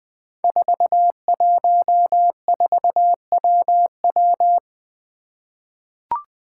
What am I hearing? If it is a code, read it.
Morse code, heard as a keyed tone, about 20 wpm, 700 Hz. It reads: 414WW
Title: Five Challenging Letter-Number Combinations 20wpm